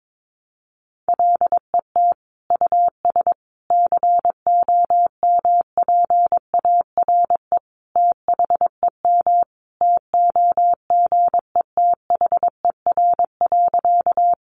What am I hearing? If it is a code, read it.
Morse code, heard as a keyed tone, about 22 wpm, 700 Hz. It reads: LET US COMPARE THEM TOGETHER.
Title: Sentences from Top 1300 Words 22wpm